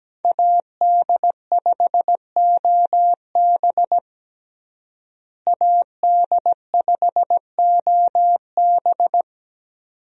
Morse code, heard as a keyed tone, about 17 wpm, 700 Hz. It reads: AD5OB AD5OB
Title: USA Call Signs - 5 Characters - Repeated 2x - 17wpm